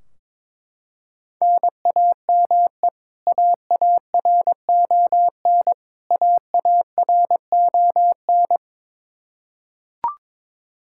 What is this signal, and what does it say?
Morse code, heard as a keyed tone, about 22 wpm, 700 Hz. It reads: NAME AARON AARON
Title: Practice QSO - Exchange 1 - Beginning - 22wpm